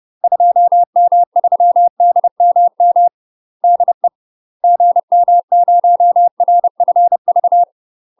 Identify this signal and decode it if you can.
Morse code, heard as a keyed tone, about 30 wpm, 700 Hz. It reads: 2M3DMM DE GM0RFV
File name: UK MASTER QSOs TRAINING/30WPM